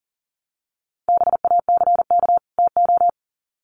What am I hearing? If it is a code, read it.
Morse code, heard as a keyed tone, about 40 wpm, 700 Hz. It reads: BACK TO
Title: Sets of 2 Words 40wpm